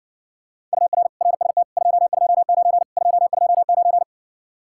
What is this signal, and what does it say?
Morse code, heard as a keyed tone, about 60 wpm, 700 Hz. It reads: UR RST 229 229